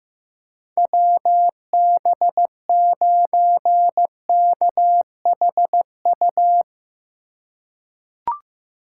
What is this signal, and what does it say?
Morse code, heard as a keyed tone, about 15 wpm, 700 Hz. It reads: WB9KHU
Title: USA Call Signs - 3x Character Spacing - 15wpm